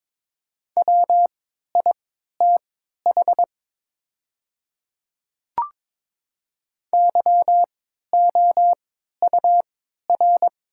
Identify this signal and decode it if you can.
Morse code, heard as a keyed tone, about 22 wpm, spaced as for 12 wpm, 690 Hz. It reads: WITH YOUR
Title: 3x Character Spacing - Top 100 Words - 22wpm